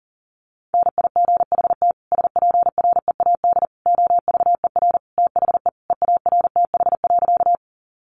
Morse code, heard as a keyed tone, about 40 wpm, 700 Hz. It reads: NIGHT SPREAD OVER THE EARTH.
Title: Sentences from Top 1200 Words 40wpm